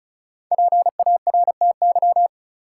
Morse code, heard as a keyed tone, about 36 wpm, 700 Hz. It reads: PARTY
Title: Sentences from Top 1400 Words - New Word Drill - 35wpm